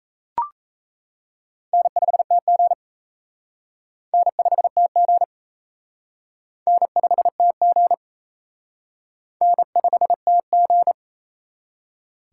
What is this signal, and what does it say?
Morse code, heard as a keyed tone, about 34 wpm, 700 Hz. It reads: N5TG N5TG N5TG N5TG
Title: Speed Racing - USA 1x2 Call Signs 28wpm